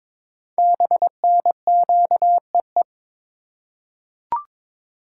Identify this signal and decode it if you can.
Morse code, heard as a keyed tone, about 22 wpm, 700 Hz. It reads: BNQEE
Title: Code Groups - 5 Letters - 22wpm